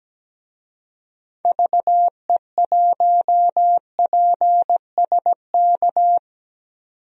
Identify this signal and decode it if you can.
Morse code, heard as a keyed tone, about 17 wpm, 700 Hz. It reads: VE1PSK
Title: DXCC-100 Call Signs - 6 Characters - 17wpm